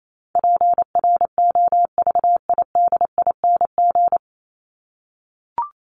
Morse code, heard as a keyed tone, about 28 wpm, 700 Hz. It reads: PROVIDING
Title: ING Suffix - Top 100 Words with Base Words 28wpm